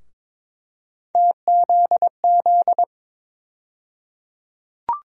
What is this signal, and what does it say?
Morse code, heard as a keyed tone, about 22 wpm, 700 Hz. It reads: TZZ
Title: USA Call Sign Postfixes - 3 Letters 22wpm